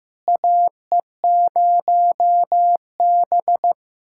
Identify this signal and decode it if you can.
Morse code, heard as a keyed tone, about 15 wpm, 700 Hz. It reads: AE0B
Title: USA Callsign DE Callsign - 4 Characters - 15wpm